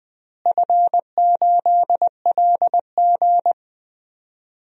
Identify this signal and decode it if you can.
Morse code, heard as a keyed tone, about 20 wpm, 700 Hz. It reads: F8LG
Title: DXCC-100 Call Signs - 4 Characters - Repeated 2x - 20wpm